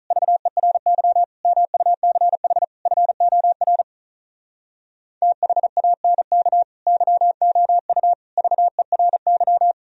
Morse code, heard as a keyed tone, about 46 wpm, 700 Hz. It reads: VERY MUCH FOR THANK YOU VERY